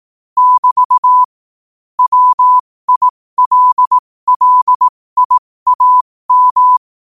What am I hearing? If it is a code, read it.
Morse code, heard as a keyed tone, about 18 wpm, spaced as for 15 wpm, 1000 Hz. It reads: <BT> WILLIAM